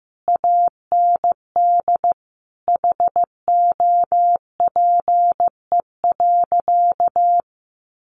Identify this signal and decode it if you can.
Morse code, heard as a keyed tone, about 15 wpm, 700 Hz. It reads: AND HOPE.
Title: Sentences from Top 1000 Words 15wpm